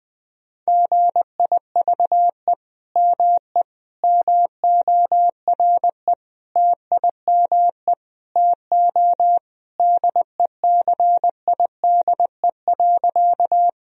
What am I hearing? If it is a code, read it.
Morse code, heard as a keyed tone, about 21 wpm, 700 Hz. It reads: GIVE ME MORE TIME TO DECIDE.
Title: Sentences from Top 1000 Words - Review Part 3 - 20wpm